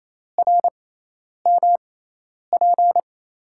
Morse code, heard as a keyed tone, about 28 wpm, spaced as for 10 wpm, 700 Hz. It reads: RMP
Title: USA Call Signs - 5 Characters - 6x Character Spacing - 28wpm